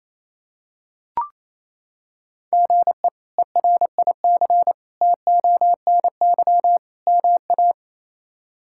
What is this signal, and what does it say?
Morse code, heard as a keyed tone, about 28 wpm, 700 Hz. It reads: GE ERIC TONY MA
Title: K1USN SST QSO Practice - 28wpm